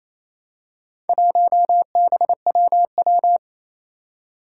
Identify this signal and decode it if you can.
Morse code, heard as a keyed tone, about 27 wpm, 700 Hz. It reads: 1BWW
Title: Four Challenging Letter-Number Combinations 28wpm